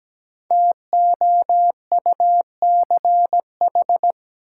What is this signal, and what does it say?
Morse code, heard as a keyed tone, about 17 wpm, 700 Hz. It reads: TOUCH
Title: Sentences from Top 1200 Words - New Word Drill - 17wpm